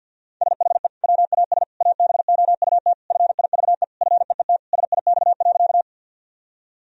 Morse code, heard as a keyed tone, about 52 wpm, 700 Hz. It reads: SHE WAS ABOUT FIVE FEET SIX.
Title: Sentences from Top 1000 Words - Review Part 2 - 50wpm